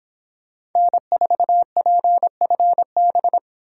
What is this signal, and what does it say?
Morse code, heard as a keyed tone, about 25 wpm, 700 Hz. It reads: N4PFB